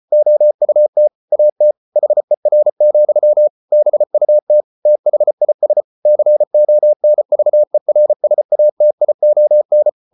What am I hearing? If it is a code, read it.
Morse code, heard as a keyed tone, about 34 wpm, 600 Hz. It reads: OUT AT HER, BUT THIS CONVERSATION